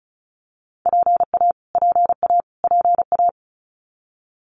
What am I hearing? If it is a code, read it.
Morse code, heard as a keyed tone, about 34 wpm, 700 Hz. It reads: PA PA PA
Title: US States Repeated 3x 35wpm